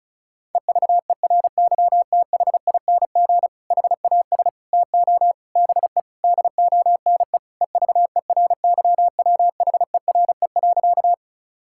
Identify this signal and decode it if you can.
Morse code, heard as a keyed tone, about 36 wpm, 700 Hz. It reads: EVERYTHING HAS TO BE DONE EVERYWHERE.
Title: Sentences from Top 1600 Words 35wpm